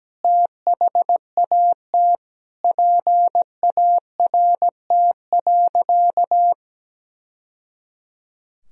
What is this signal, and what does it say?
Morse code, heard as a keyed tone, about 17 wpm, 700 Hz. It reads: THAT PART.
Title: Sentences from Top 1000 Words - Review Part 1 - 17wpm